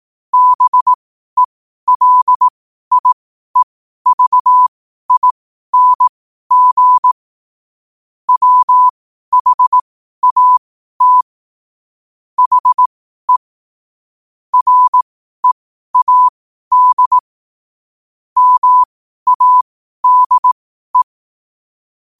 Morse code, heard as a keyed tone, about 18 wpm, spaced as for 12 wpm, 1000 Hz. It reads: BELIEVING WHAT HE READ MADE
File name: qotdfiles / 12wpm